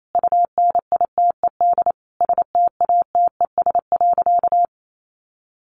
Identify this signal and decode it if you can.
Morse code, heard as a keyed tone, about 28 wpm, 700 Hz. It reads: UNITED STATES.
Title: A Sound Of Thunder 28wpm